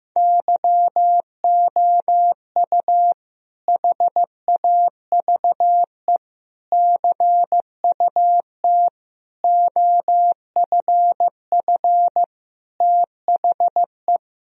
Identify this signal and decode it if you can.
Morse code, heard as a keyed tone, about 15 wpm, 700 Hz. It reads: YOU HAVE CUT OFF THE
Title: Sentences from Top 1000 Words - Review Part 1 - 15wpm